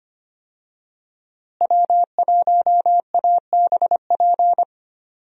Morse code, heard as a keyed tone, about 25 wpm, 700 Hz. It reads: W1ABP
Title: USA Call Signs - 5 Characters - Repeated 2x - 25wpm